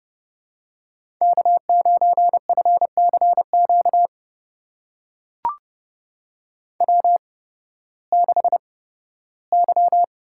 Morse code, heard as a keyed tone, about 30 wpm, spaced as for 16 wpm, 700 Hz. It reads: K9FCQ W6Y
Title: USA Call Signs - 5 Characters - 8x Character Spacing - 30wpm